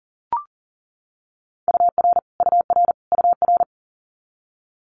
Morse code, heard as a keyed tone, about 40 wpm, 700 Hz. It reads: UR UR UR
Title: Abbreviations Repeated 3x - 40wpm